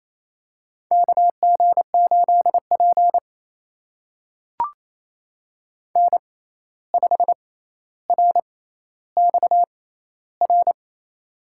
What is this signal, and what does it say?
Morse code, heard as a keyed tone, about 28 wpm, spaced as for 13 wpm, 700 Hz. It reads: KG8P N5RXR
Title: USA Call Signs - 6x Character Spacing - 28wpm